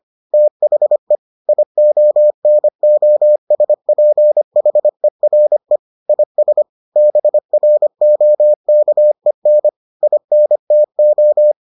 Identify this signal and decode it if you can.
Morse code, heard as a keyed tone, about 25 wpm, 600 Hz. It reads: THE IONOSPHERE IS BROKEN INTO